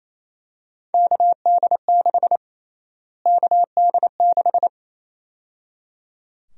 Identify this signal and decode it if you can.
Morse code, heard as a keyed tone, about 28 wpm, 700 Hz. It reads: KD6 KD6
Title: USA Call Sign Prefixes - Repeated 2x - 28wpm